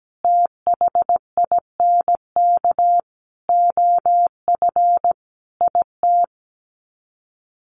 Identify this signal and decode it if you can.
Morse code, heard as a keyed tone, about 17 wpm, 700 Hz. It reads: THINK OF IT
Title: Sets of 5 Words Encore 17wpm